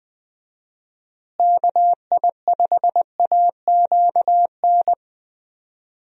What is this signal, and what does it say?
Morse code, heard as a keyed tone, about 21 wpm, 700 Hz. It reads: KI5AQN
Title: USA Call Signs - 6 Characters - Repeated 2x - 20wpm